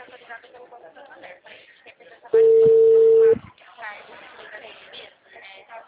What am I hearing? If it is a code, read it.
Ringback tone (China, 450 Hz), with speech.
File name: out- -2006